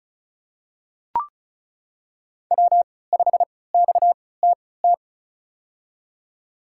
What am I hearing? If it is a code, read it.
Morse code, heard as a keyed tone, about 34 wpm, spaced as for 20 wpm, 690 Hz. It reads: W5XTT
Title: USA Call Signs - 3x Character Spacing - 35wpm